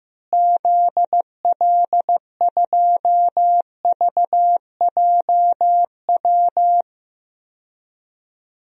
Morse code, heard as a keyed tone, about 15 wpm, 700 Hz. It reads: ZL2VJW
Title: DXCC-100 Call Signs - 6 Characters - Repeated 3x - 15wpm